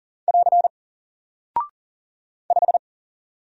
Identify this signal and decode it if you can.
Morse code, heard as a keyed tone, about 40 wpm, 700 Hz. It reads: <AR> 5
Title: Single Character (Letter:Number:Punctuation:Prosign) Alt 40wpm